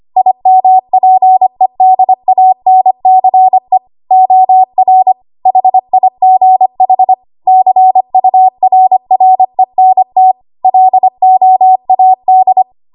Morse code, heard as a keyed tone, about 25 wpm, 750 Hz. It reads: IMPEDANCE OR HIGH CURRENT LOAD
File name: Archive / 25 WPM / 25WPM